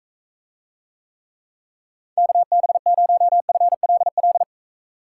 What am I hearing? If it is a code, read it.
Morse code, heard as a keyed tone, about 43 wpm, 700 Hz. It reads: KB0FLL